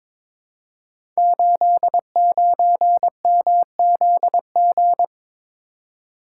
Morse code, heard as a keyed tone, about 22 wpm, 700 Hz. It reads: 89MZG
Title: Code Groups - 5 Letter:Numbers - 22wpm